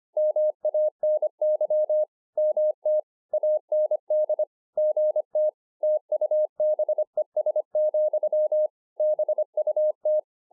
Morse code, heard as a keyed tone, about 25 wpm, 610 Hz. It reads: MANY MT AND GT TUBES, BUT